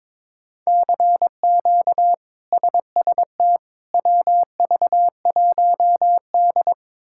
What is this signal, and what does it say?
Morse code, heard as a keyed tone, about 22 wpm, 700 Hz. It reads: CQ SST WV1D
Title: K1USN SST QSO Practice - 22wpm